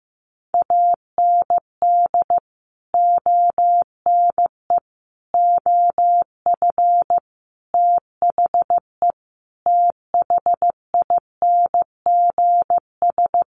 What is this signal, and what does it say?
Morse code, heard as a keyed tone, about 15 wpm, 700 Hz. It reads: AND ONE OF THE THINGS
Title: Sets of 5 Words 15wpm